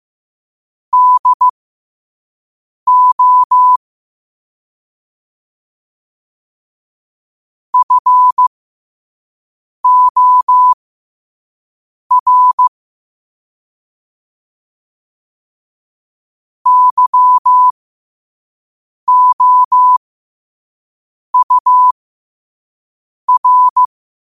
Morse code, heard as a keyed tone, about 15 wpm, spaced as for 5 wpm, 1000 Hz. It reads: DO FOR YOUR
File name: qotdfiles / 5wpm